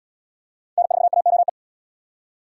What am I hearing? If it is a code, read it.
Morse code, heard as a keyed tone, about 77 wpm, 700 Hz. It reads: N4AOE